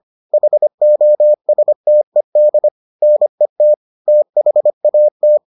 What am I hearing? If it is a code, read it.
Morse code, heard as a keyed tone, about 25 wpm, 600 Hz. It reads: HOSTED NET THAT